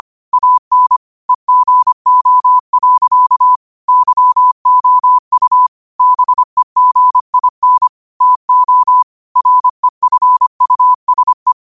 Morse code, heard as a keyed tone, about 25 wpm, 1000 Hz. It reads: AN EGO. YOU BEGIN TO REFUSE
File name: qotdfiles / 25wpm